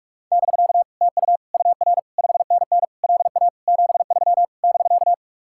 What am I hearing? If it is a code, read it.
Morse code, heard as a keyed tone, about 49 wpm, 700 Hz. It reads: <BK> TU UR 5NN LA 73 <BK>